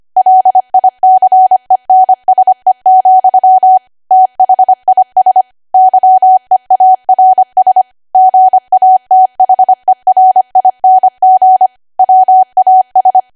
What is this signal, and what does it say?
Morse code, heard as a keyed tone, about 25 wpm, 750 Hz. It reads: LICENSE, THIS YEARS GATHERING WAS
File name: Archive / 25 WPM / 25WPM